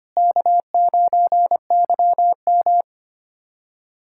Morse code, heard as a keyed tone, about 25 wpm, 700 Hz. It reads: K9YM
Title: USA Callsign DE Callsign - 4 Characters - 25wpm